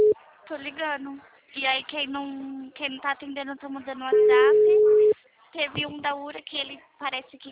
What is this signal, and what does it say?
Ringback tone (Europe, 425 Hz), with speech.
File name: out- -2007- - - .6468